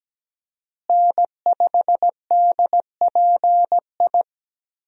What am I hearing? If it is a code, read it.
Morse code, heard as a keyed tone, about 17 wpm, 700 Hz. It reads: N5DPI
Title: USA Call Signs - 5 Characters - Repeated 3x - 17wpm